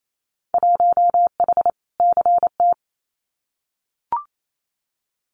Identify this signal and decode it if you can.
Morse code, heard as a keyed tone, about 28 wpm, 700 Hz. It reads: 1H CT
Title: Winter Field Day Exchanges 28wpm